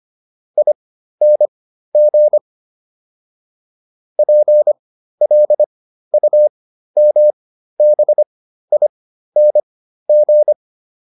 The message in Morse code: ING PLUMBING

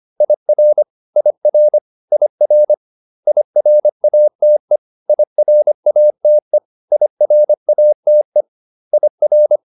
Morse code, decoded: IR IR IR IRATE IRATE IRATE IR

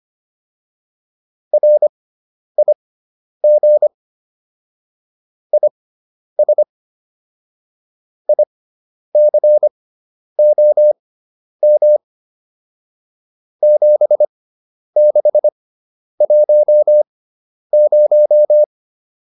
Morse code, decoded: RIG IS ICOM 7610

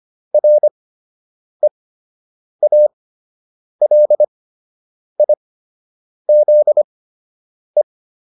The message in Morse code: REALIZE

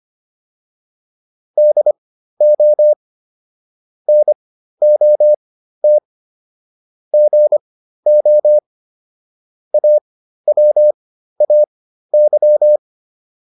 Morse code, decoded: DO NOT GO AWAY

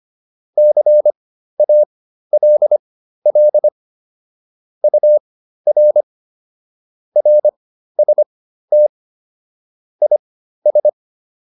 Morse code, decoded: CALL UR RST IS

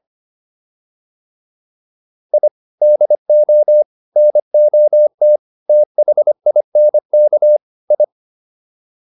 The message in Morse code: I DO NOT THINK I

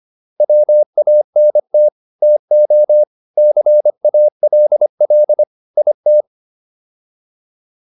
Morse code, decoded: WANT TO CALL IT

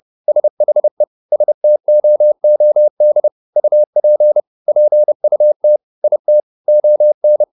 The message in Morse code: SHE STOOD UP PUT IT ON

